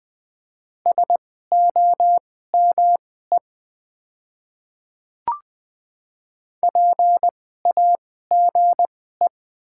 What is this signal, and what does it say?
Morse code, heard as a keyed tone, about 20 wpm, spaced as for 14 wpm, 690 Hz. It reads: SOME PAGE
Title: 2x Character Spacing - Top 500 Words - 20wpm